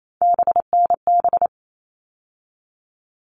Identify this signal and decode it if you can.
Morse code, heard as a keyed tone, about 28 wpm, 700 Hz. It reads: BNB
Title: All Cut Numbers - 3-Digit 28wpm